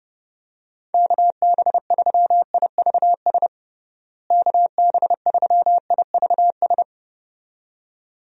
Morse code, heard as a keyed tone, about 30 wpm, 700 Hz. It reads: KB3IVS KB3IVS
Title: USA Call Signs - 6 Characters - Repeated 2x - 30wpm